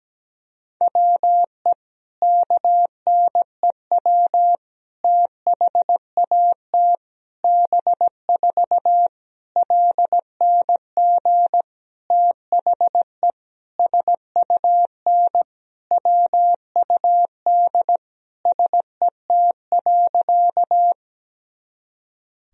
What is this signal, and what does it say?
Morse code, heard as a keyed tone, about 17 wpm, 700 Hz. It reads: WE KNEW THAT B4 LNG THE SUN WUD SET.
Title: Sentences from Top 400 Words and Abbreviations 17wpm